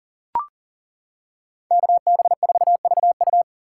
Morse code, heard as a keyed tone, about 40 wpm, 700 Hz. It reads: KB4VU
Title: K1USN SST QSO Practice - 40wpm